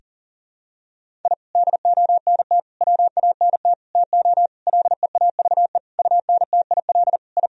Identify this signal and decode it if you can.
Morse code, heard as a keyed tone, about 37 wpm, 700 Hz. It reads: I DONT WANT TO LEAVE UNTIL I
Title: Sentences from Top 1000 Words - Review Part 1 - 40wpm